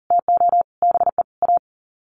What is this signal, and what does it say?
Morse code, heard as a keyed tone, about 40 wpm, 700 Hz. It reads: TO BE A
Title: Sets of 5 Words 40wpm